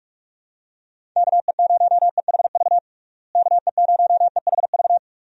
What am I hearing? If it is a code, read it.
Morse code, heard as a keyed tone, about 46 wpm, 700 Hz. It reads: KE0EHV KE0EHV
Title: USA Call Signs - 6 Characters - Repeated 2x - 45wpm